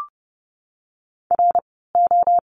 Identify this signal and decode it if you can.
Morse code, heard as a keyed tone, about 30 wpm, 700 Hz. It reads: R O
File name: Lesson 16f - Letter c - Words 30wpm